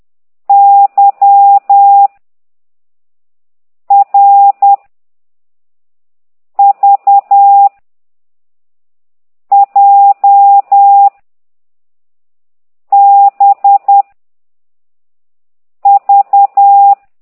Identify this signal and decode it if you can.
Morse code, heard as a keyed tone, about 10 wpm, 800 Hz. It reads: Y R V J B V
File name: ct2idw / exerc9